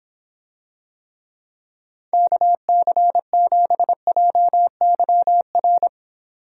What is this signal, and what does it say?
Morse code, heard as a keyed tone, about 25 wpm, 700 Hz. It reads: KC7JYR